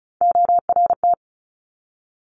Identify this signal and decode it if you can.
Morse code, heard as a keyed tone, about 36 wpm, 700 Hz. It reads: ORT
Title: Top Three Letter Combinations in Words 35wpm